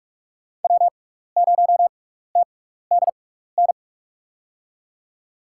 Morse code, heard as a keyed tone, about 47 wpm, spaced as for 16 wpm, 700 Hz. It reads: W0TDN
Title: USA Call Signs - 5 Characters - 6x Character Spacing - 45wpm